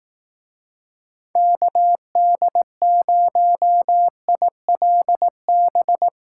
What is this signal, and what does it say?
Morse code, heard as a keyed tone, about 18 wpm, 700 Hz. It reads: KD0ILB
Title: Speed Racing - USA 2x3 Call Signs 15wpm